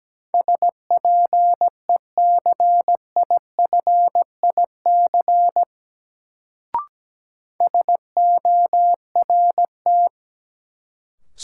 Morse code, heard as a keyed tone, about 17 wpm, 700 Hz. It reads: SPECIFIC SORT
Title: Sentences from Top 1300 Words - New Word Drill Alt - 17wpm